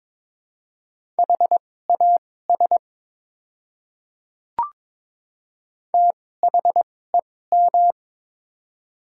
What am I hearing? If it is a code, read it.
Morse code, heard as a keyed tone, about 22 wpm, spaced as for 16 wpm, 700 Hz. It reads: HAS THEM
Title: 2x Character Spacing - Top 100 Words - 22wpm